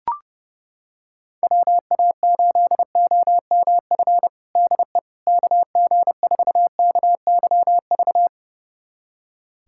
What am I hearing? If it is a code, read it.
Morse code, heard as a keyed tone, about 32 wpm, 700 Hz. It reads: WA8OMF DE KG4KYV
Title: USA Callsign DE Callsign - 6 Characters - 30wpm